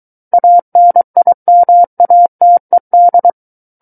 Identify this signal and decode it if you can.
Morse code, heard as a keyed tone, about 23 wpm, 700 Hz. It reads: ANIMATED
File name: Book One / 14-Chapter 14 - 23 WPM